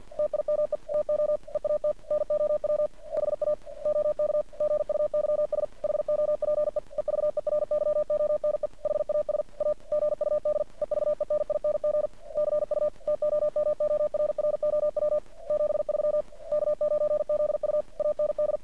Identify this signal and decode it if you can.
Morse code, heard as a keyed tone, about 50 wpm, 600 Hz. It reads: TIME TO EAT NOW HA OK GUYS HOPE EVERYONE HAS A GUD EVENING CU TOMORROW 73 K9ZU AND
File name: archived / CRSnet